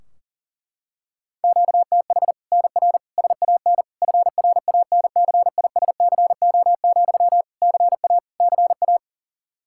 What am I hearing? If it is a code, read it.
Morse code, heard as a keyed tone, about 37 wpm, 700 Hz. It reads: QTH NR SAN FRANCISCO, CA CA
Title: Practice QSO - Exchange 1 - Beginning - 40wpm